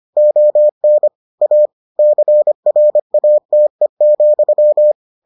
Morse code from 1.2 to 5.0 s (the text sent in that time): A CRATE,